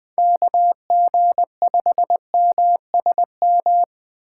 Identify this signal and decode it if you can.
Morse code, heard as a keyed tone, about 20 wpm, 700 Hz. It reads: KG5MSM
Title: USA Call Signs - 2x Character Spacing - 20wpm